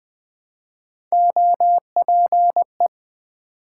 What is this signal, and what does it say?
Morse code, heard as a keyed tone, about 20 wpm, 700 Hz. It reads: OPE
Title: Build A Word - 1 Letter At A Time - 5-Letter Words - 20wpm